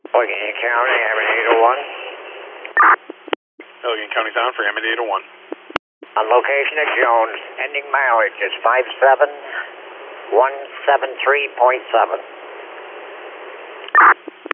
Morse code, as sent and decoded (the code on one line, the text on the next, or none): none